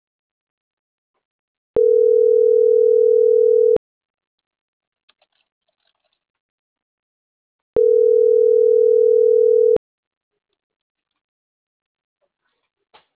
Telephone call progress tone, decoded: ringback tone

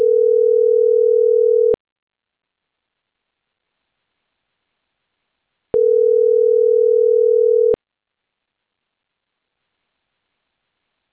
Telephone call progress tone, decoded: ringback tone

